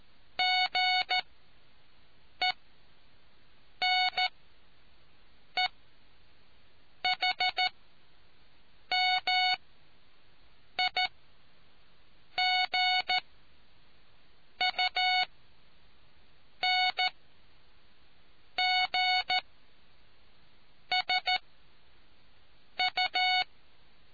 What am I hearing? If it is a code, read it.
Morse code, heard as a keyed tone, about 14 wpm, spaced as for 5 wpm, 740 Hz. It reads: GENEHMIGUNGSU